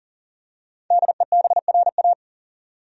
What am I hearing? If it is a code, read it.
Morse code, heard as a keyed tone, about 40 wpm, 700 Hz. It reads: DEBRA
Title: USA Common Names - 40wpm